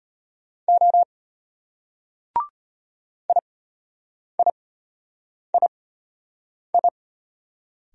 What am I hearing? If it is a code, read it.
Morse code, heard as a keyed tone, about 34 wpm, spaced as for 8 wpm, 690 Hz. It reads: O IIII